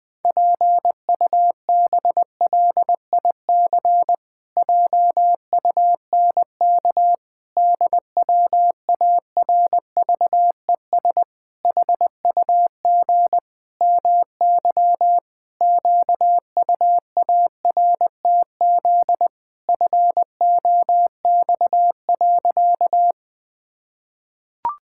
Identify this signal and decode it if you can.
Morse code, heard as a keyed tone, about 20 wpm, 700 Hz. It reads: PUBLIC JUNK DWARVES HUG MY QUARTZ FOX.
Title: Pangram Challenge v2 - 20wpm